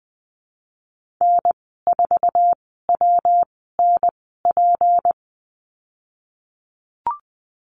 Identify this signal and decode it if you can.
Morse code, heard as a keyed tone, about 20 wpm, spaced as for 14 wpm, 700 Hz. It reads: N4WNP
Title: 179 Lesson 29g - Slash - Call Signs 20wpm